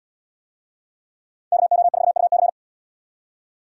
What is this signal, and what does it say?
Morse code, heard as a keyed tone, about 71 wpm, 700 Hz. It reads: KO4UC